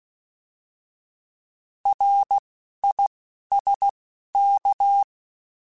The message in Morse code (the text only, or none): RISK